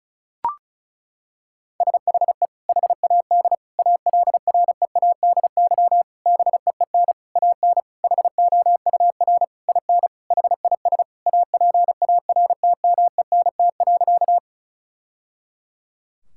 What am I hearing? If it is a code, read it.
Morse code, heard as a keyed tone, about 36 wpm, 700 Hz. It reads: SHE HAD ALREADY BEEN AN HOUR IN HIS APARTMENT.
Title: Sentences from Top 1000 Words - Review Part 6 - 35wpm